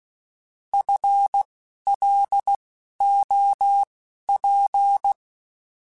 Morse code, heard as a keyed tone, about 16 wpm, spaced as for 11 wpm, 800 Hz. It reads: FLOP
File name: FL Runs / FL Runs mp3s / FL Runs @10